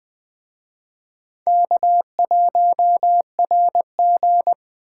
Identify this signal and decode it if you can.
Morse code, heard as a keyed tone, about 20 wpm, 700 Hz. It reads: K1RG